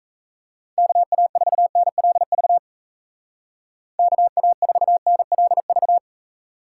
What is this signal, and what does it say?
Morse code, heard as a keyed tone, about 33 wpm, 700 Hz. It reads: KA4NLV KA4NLV
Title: Speed Racing - USA 2x3 Call Signs 28wpm